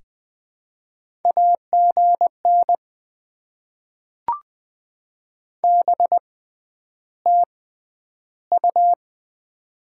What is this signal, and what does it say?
Morse code, heard as a keyed tone, about 20 wpm, spaced as for 10 wpm, 700 Hz. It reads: AGN BTU
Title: Top 20 QSO Elements - 6x Character Spacing - 20wpm